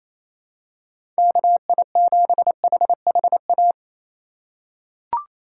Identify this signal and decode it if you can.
Morse code, heard as a keyed tone, about 28 wpm, 700 Hz. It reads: KI7HHA
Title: USA Call Signs - 6 Characters - 6x Character Spacing - 28wpm